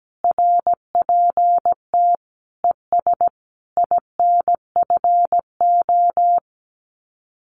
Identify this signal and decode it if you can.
Morse code, heard as a keyed tone, about 17 wpm, 700 Hz. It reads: RPT ES INFO
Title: Common QSO Phrases Encore 17wpm